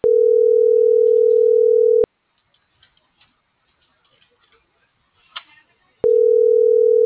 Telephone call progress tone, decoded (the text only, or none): ringback tone